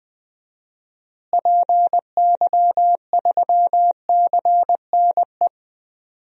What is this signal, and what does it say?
Morse code, heard as a keyed tone, about 20 wpm, 700 Hz. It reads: PY3CNE
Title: DXCC-100 Call Signs - 6 Characters - Repeated 3x - 20wpm